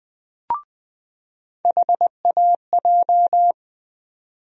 Morse code, heard as a keyed tone, about 20 wpm, 700 Hz. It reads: HAJ
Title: USA Call Sign Postfixes - 3 Letters 20wpm